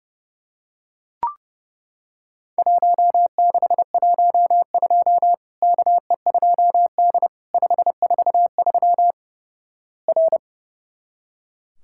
Morse code, heard as a keyed tone, about 30 wpm, 700 Hz. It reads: 1612 KE2D 543 R
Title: Worked All Europe DX Contest - QTC Exchange - 30wpm